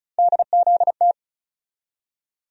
Morse code, heard as a keyed tone, about 36 wpm, 700 Hz. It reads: DZT
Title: USA Call Sign Postfixes - 3 Letters 35wpm